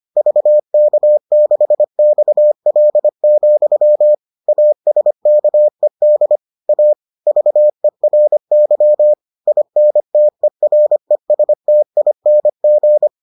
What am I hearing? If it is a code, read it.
Morse code, heard as a keyed tone, about 25 wpm, 600 Hz. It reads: VK6XL, ASKED A VERY INTERESTING